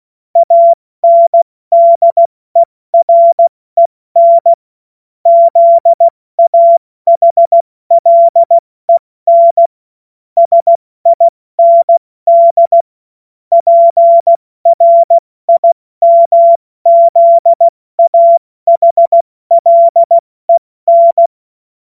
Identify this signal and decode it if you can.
Morse code, heard as a keyed tone, about 16 wpm, spaced as for 14 wpm, 680 Hz. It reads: ANDEREN ZAHLEN SIND PRIMZAHLEN
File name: Morsekurs / Witze / Witz 12. April 2022 70 F80